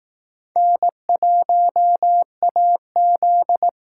Morse code, heard as a keyed tone, about 18 wpm, 700 Hz. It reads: N1AZ
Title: Speed Racing - USA 1x2 Call Signs 15wpm